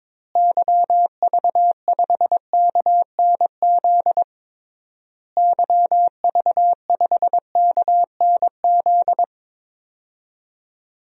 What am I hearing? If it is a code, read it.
Morse code, heard as a keyed tone, about 22 wpm, 700 Hz. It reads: YV5KNZ YV5KNZ
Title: DXCC-100 Call Signs - 6 Characters - Repeated 3x - 22wpm